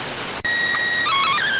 SSTV (slow-scan television), PD 290.